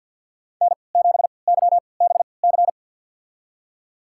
Morse code, heard as a keyed tone, about 51 wpm, spaced as for 29 wpm, 700 Hz. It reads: N7YBC
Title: USA Call Signs - 5 Characters - 3x Character Spacing - 50wpm